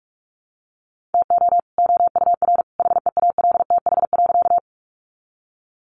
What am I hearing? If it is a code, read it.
Morse code, heard as a keyed tone, about 44 wpm, 700 Hz. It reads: TO OUR HEALTH.
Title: Sentences from Top 600 Words 45wpm